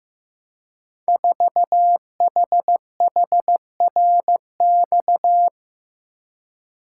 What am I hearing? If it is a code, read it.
Morse code, heard as a keyed tone, about 15 wpm, 700 Hz. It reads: 4HHRX
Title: Code Groups - 5 Letter:Numbers - 15wpm